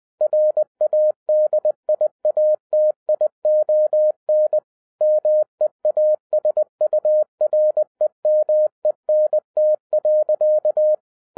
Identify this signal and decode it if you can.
Morse code, heard as a keyed tone, about 20 wpm, 610 Hz. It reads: RADIATION MEASUREMENT.
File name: 2011-10-mp3 / n03-TARO JR0QWW 20wpm